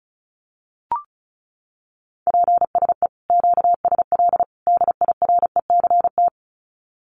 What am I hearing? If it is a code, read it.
Morse code, heard as a keyed tone, about 34 wpm, 700 Hz. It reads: PSE QSL DIRECT
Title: Common QSO Phrases 35wpm